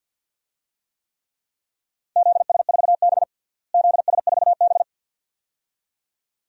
Morse code, heard as a keyed tone, about 52 wpm, 700 Hz. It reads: ZS4B ZS4B
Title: DXCC-100 Call Signs - 4 Characters - Repeated 2x - 50wpm